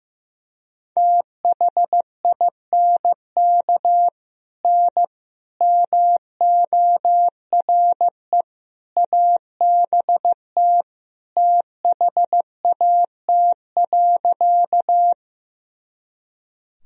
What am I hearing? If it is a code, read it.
Morse code, heard as a keyed tone, about 15 wpm, 700 Hz. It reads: THINK N MORE ABT THAT.
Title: Sentences from Top 200 Words and Abbreviations 15wpm